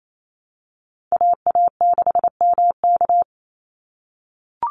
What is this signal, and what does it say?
Morse code, heard as a keyed tone, about 28 wpm, 700 Hz. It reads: AA6MK
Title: Call Signs - 28wpm